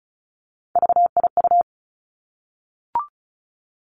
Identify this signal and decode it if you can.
Morse code, heard as a keyed tone, about 36 wpm, 700 Hz. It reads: VIU
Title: Three Challenging Letter-Number Combinations 35wpm